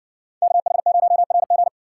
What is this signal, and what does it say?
Morse code, heard as a keyed tone, about 55 wpm, 700 Hz. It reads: KH0RG